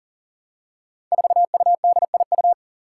Morse code, heard as a keyed tone, about 40 wpm, 700 Hz. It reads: 4UDIU
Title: Five Challenging Letter-Number Combinations 40wpm